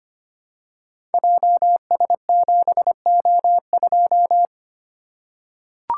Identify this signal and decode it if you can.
Morse code, heard as a keyed tone, about 25 wpm, 700 Hz. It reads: JS7O2
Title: Code Groups - 5 Letter:Numbers - 25wpm